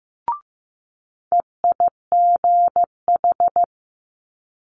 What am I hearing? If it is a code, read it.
Morse code, heard as a keyed tone, about 15 wpm, 700 Hz. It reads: EIGH
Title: Multi-letter Phonemes Quick Fire 15wpm